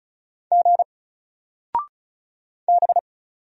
Morse code, heard as a keyed tone, about 35 wpm, 700 Hz. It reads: G B
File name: Single-Letter-Number-35wpm